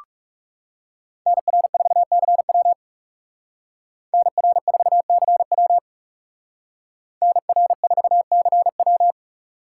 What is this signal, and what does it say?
Morse code, heard as a keyed tone, about 38 wpm, 700 Hz. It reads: NR4CW NR4CW NR4CW